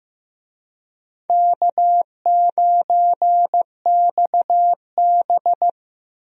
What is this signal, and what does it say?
Morse code, heard as a keyed tone, about 15 wpm, 700 Hz. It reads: K9XB
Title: USA Call Signs - 4 Characters - 15wpm